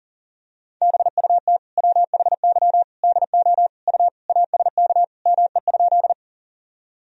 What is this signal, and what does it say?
Morse code, heard as a keyed tone, about 37 wpm, 700 Hz. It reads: BUT WHY DO U ASK ME?
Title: Sentences from Top 200 Words and Abbreviations 40wpm